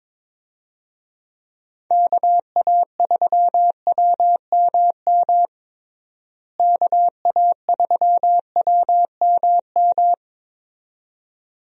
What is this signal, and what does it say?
Morse code, heard as a keyed tone, about 23 wpm, 700 Hz. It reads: KA3WMM KA3WMM
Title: USA Call Signs - 6 Characters - Repeated 2x - 22wpm